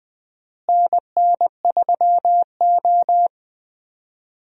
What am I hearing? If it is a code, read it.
Morse code, heard as a keyed tone, about 21 wpm, 700 Hz. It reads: NN3O
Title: USA Call Signs - 4 Characters - Repeated 3x - 20wpm